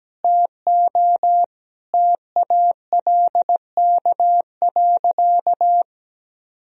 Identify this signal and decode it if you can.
Morse code, heard as a keyed tone, about 17 wpm, 700 Hz. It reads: TO TALK.
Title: Sentences from Top 1000 Words - Review Part 1 - 17wpm